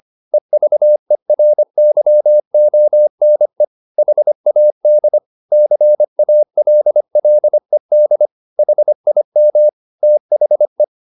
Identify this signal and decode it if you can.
Morse code, heard as a keyed tone, about 25 wpm, 600 Hz. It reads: EVERYONE HAD CALLED HIM THE